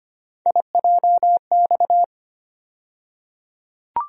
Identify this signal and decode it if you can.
Morse code, heard as a keyed tone, about 25 wpm, 700 Hz. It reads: IJX
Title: USA Call Sign Postfixes - 3 Letters 25wpm